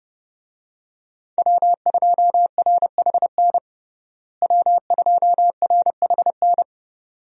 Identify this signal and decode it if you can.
Morse code, heard as a keyed tone, about 30 wpm, 700 Hz. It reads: W2RHN W2RHN
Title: USA Call Signs - 5 Characters - Repeated 2x - 30wpm